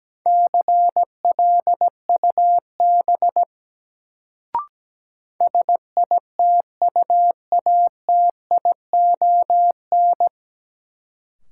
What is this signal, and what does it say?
Morse code, heard as a keyed tone, about 17 wpm, 700 Hz. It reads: CLUB SITUATION
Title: Sentences from Top 1200 Words - New Word Drill - 17wpm